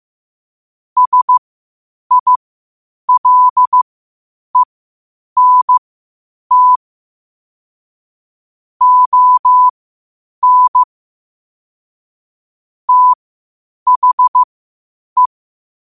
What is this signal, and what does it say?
Morse code, heard as a keyed tone, about 15 wpm, spaced as for 8 wpm, 1000 Hz. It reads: SILENT ON THE
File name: qotdfiles / 8wpm